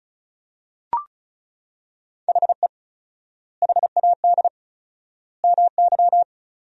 Morse code, heard as a keyed tone, about 34 wpm, 700 Hz. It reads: HE HAD MY
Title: 4x Word-Spacing - Sentences from Top 400 Words - 35wpm